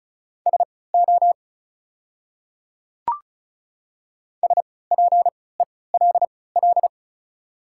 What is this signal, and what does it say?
Morse code, heard as a keyed tone, about 34 wpm, spaced as for 20 wpm, 700 Hz. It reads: SO SPELL
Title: 3x Character Spacing - Top 500 Words - 35wpm